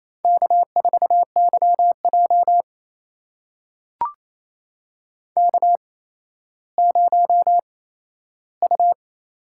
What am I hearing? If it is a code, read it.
Morse code, heard as a keyed tone, about 28 wpm, spaced as for 14 wpm, 700 Hz. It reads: K4YJ K0U
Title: USA Call Signs - 4 Characters - 8x Character Spacing - 28wpm